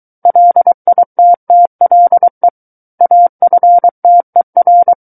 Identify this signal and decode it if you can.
Morse code, heard as a keyed tone, about 23 wpm, 700 Hz. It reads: LITTLE AFTER
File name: Book One / 14-Chapter 14 - 23 WPM